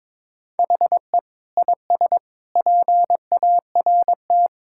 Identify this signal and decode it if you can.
Morse code, heard as a keyed tone, about 22 wpm, 700 Hz. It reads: HE IS PART
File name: practice / 22wpm / Sentences with 3-Word Phrases - 22wpm